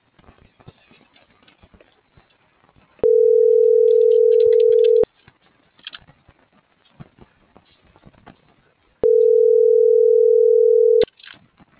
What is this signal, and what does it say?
Ringback tone (North America), 440 + 480 Hz.